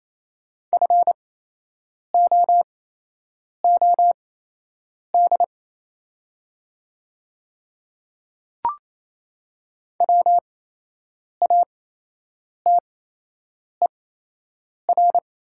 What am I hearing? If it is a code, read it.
Morse code, heard as a keyed tone, about 28 wpm, spaced as for 8 wpm, 690 Hz. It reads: FOOD WATER
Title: 8x Character Spacing - Top 500 Words - 28wpm